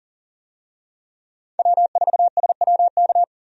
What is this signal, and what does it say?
Morse code, heard as a keyed tone, about 40 wpm, 700 Hz. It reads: W4SWK
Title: USA Call Signs - 5 Characters - 40wpm